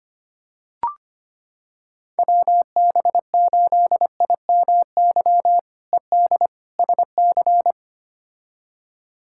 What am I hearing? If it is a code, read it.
Morse code, heard as a keyed tone, about 25 wpm, 700 Hz. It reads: WB8IMY ED SC
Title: K1USN SST QSO Practice - 25wpm